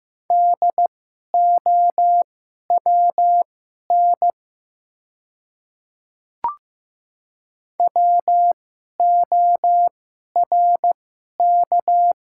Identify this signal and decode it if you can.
Morse code, heard as a keyed tone, about 15 wpm, spaced as for 11 wpm, 700 Hz. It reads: DOWN WORK
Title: 2x Character Spacing - Top 200 Words - 15wpm